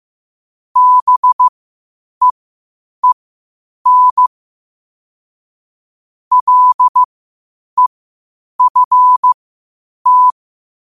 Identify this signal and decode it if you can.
Morse code, heard as a keyed tone, about 15 wpm, spaced as for 8 wpm, 1000 Hz. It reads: BEEN LEFT
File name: qotdfiles / 8wpm